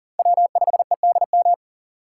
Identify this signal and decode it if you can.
Morse code, heard as a keyed tone, about 40 wpm, 700 Hz. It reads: W5EDM
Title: USA Call Signs - 5 Characters - 3x Character Spacing - 40wpm